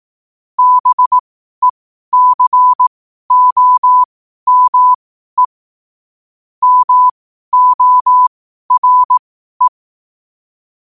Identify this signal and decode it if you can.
Morse code, heard as a keyed tone, about 18 wpm, spaced as for 12 wpm, 1000 Hz. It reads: BECOME MORE
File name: qotdfiles / 12wpm